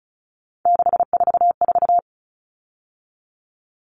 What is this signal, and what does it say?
Morse code, heard as a keyed tone, about 34 wpm, 700 Hz. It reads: 644
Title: All Cut Numbers - 3-Digit 35wpm